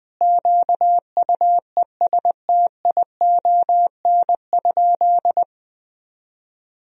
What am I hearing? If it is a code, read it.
Morse code, heard as a keyed tone, about 20 wpm, 700 Hz. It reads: QUESTION?
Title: Sentences from Top 1000 Words - Review Part 3 - 20wpm